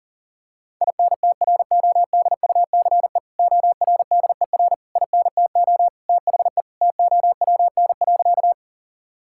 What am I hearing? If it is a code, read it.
Morse code, heard as a keyed tone, about 40 wpm, 700 Hz. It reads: INTRODUCE ORDER INTO THE TOWN.
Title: Sentences from Top 1600 Words 40wpm